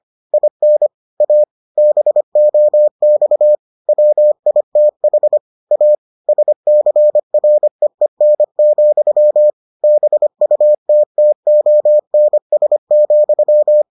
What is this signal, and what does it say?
Morse code, heard as a keyed tone, about 25 wpm, 600 Hz. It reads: IN A BOX WITH A SCREEN, BUTTONS,